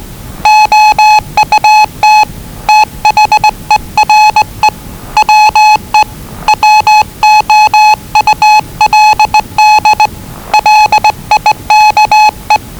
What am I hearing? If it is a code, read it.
Morse code, heard as a keyed tone, about 20 wpm, 850 Hz. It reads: OUT THERE WE WOULD LIKE
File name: WellingtonRadioZlwCloseDownOnHfAndMf / 01-Zlw 4277khz 417.5 qrt until 0925z